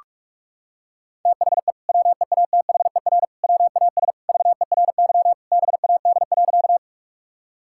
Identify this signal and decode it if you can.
Morse code, heard as a keyed tone, about 44 wpm, 700 Hz. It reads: THE WEATHER WAS VERY BAD.
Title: Sentences from Top 1000 Words - Review Part 5 - 45wpm